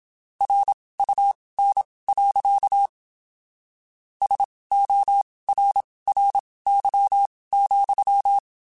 Morse code, heard as a keyed tone, about 27 wpm, spaced as for 19 wpm, 800 Hz. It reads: RUN. SORRY,
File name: (spaced) / QSOs Spaced-mp3s / kc7haa @25 sp